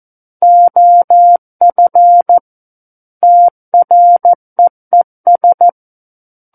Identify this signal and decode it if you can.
Morse code, heard as a keyed tone, about 14 wpm, 700 Hz. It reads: OF TREES
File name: Book One / 05-Chapter 5 - 14 WPM